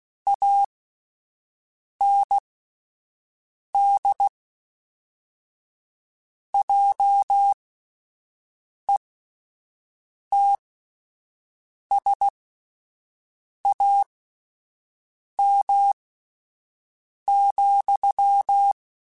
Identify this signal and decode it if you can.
Morse code, heard as a keyed tone, about 16 wpm, spaced as for 5 wpm, 800 Hz. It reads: AND JETSAM,
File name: Misc / Misc-mp3s / Word Pairs @5